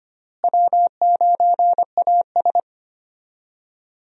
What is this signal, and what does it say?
Morse code, heard as a keyed tone, about 25 wpm, 700 Hz. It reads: W9AS
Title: USA Call Signs - 4 Characters - Repeated 3x - 25wpm